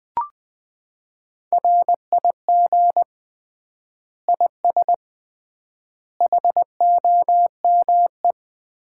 Morse code, heard as a keyed tone, about 21 wpm, 700 Hz. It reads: RIG IS HOME
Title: Common QSO Phrases - 3x Word-Spacing - 20wpm